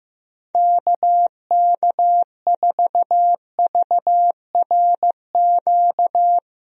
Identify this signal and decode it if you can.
Morse code, heard as a keyed tone, about 15 wpm, 700 Hz. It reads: KK4VRQ
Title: USA Call Signs - 6 Characters - Repeated 3x - 15wpm